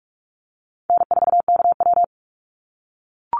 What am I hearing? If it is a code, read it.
Morse code, heard as a keyed tone, about 46 wpm, 700 Hz. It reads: N4KW
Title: Call Signs - 45wpm